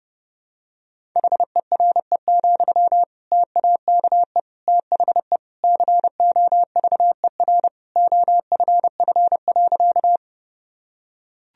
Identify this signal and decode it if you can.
Morse code, heard as a keyed tone, about 30 wpm, 700 Hz. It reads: HERE, TAKE THE COVER OFF.
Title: Sentences from Top 1000 Words - Review Part 1 - 30wpm